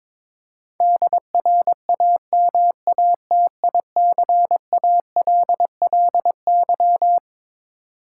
Morse code, heard as a keyed tone, about 22 wpm, 700 Hz. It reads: DRAMATICALLY
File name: practice / 22wpm / Twelve Letter Words - 22wpm